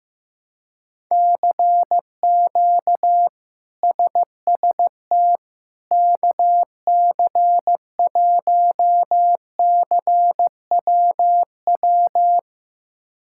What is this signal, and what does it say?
Morse code, heard as a keyed tone, about 15 wpm, 700 Hz. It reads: CQ SST KC1CWW
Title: K1USN SST QSO Practice - 15wpm